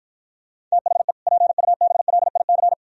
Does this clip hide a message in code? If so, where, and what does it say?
Yes: Morse code from 0.7 to 2.7 s, heard as a keyed tone, about 52 wpm, 700 Hz. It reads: THE PUBLIC